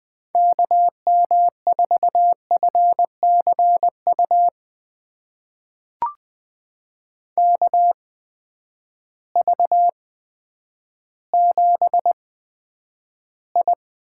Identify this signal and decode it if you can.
Morse code, heard as a keyed tone, about 20 wpm, spaced as for 10 wpm, 700 Hz. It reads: KM4FCU KV7I
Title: USA Call Signs - 8x Character Spacing - 20wpm